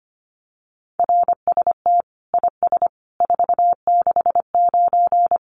Morse code, heard as a keyed tone, about 25 wpm, 700 Hz. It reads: RST IS 469
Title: Common QSO Phrases 25wpm